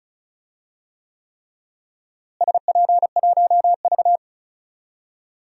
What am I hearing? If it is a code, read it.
Morse code, heard as a keyed tone, about 34 wpm, 700 Hz. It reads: SP1V
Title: DXCC-100 Call Signs - 4 Characters - Repeated 3x - 35wpm